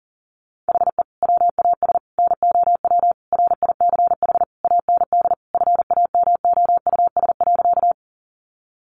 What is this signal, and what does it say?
Morse code, heard as a keyed tone, about 37 wpm, 700 Hz. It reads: HE WAS NOW RICH AND FAMOUS.
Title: Sentences from Top 1500 Words 40wpm